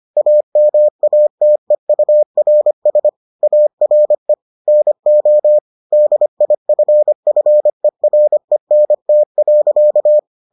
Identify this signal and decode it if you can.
Morse code, heard as a keyed tone, about 25 wpm, 600 Hz. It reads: AMATEURS ARE NO DIFFERENT.